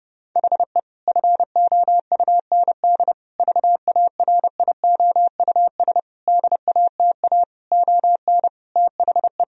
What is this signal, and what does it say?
Morse code, heard as a keyed tone, about 32 wpm, 700 Hz. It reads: HE FOUND VARIOUS DATA ON THE
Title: Sentences from Top 1000 Words - Review Part 4 - 30wpm